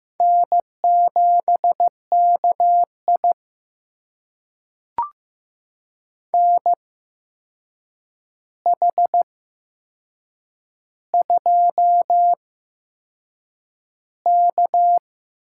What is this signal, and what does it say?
Morse code, heard as a keyed tone, about 15 wpm, spaced as for 6 wpm, 700 Hz. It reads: N7KI NH2K
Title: USA Call Signs - 4 Characters - 8x Character Spacing - 15wpm